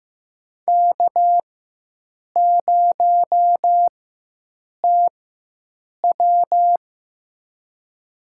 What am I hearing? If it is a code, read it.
Morse code, heard as a keyed tone, about 15 wpm, spaced as for 7 wpm, 700 Hz. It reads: K0TW
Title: USA Call Signs - 4x Character Spacing - 15wpm